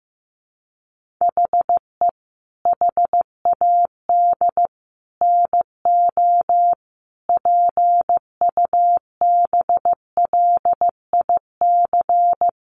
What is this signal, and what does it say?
Morse code, heard as a keyed tone, about 15 wpm, 700 Hz. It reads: HE HAD NO PUBLIC
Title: Sentences from Top 1500 Words 15wpm